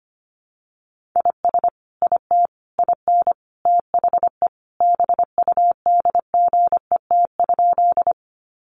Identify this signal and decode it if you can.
Morse code, heard as a keyed tone, about 25 wpm, 700 Hz. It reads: IS IT IN THE BUDGET?
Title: Sentences from Top 1200 Words 25wpm